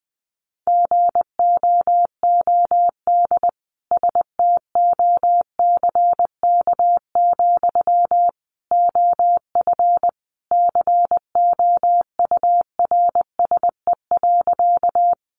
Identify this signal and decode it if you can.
Morse code, heard as a keyed tone, about 20 wpm, 700 Hz. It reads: GOOD STOCK, OF COURSE.
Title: Sentences from Top 800 Words 20wpm